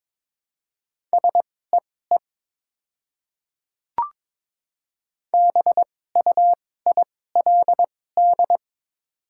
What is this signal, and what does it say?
Morse code, heard as a keyed tone, about 22 wpm, spaced as for 16 wpm, 700 Hz. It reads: SEE BUILD
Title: 2x Character Spacing - Top 200 Words - 22wpm